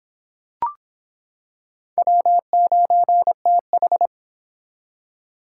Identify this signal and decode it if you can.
Morse code, heard as a keyed tone, about 26 wpm, 700 Hz. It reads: W9TH